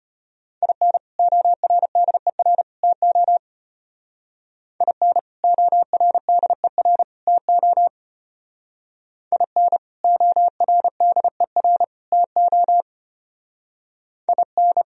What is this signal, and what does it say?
Morse code, heard as a keyed tone, about 30 wpm, 700 Hz. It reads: IN ORDER TO IN ORDER TO IN ORDER TO IN
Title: Speed-Racing - Sets of 3 Words 25wpm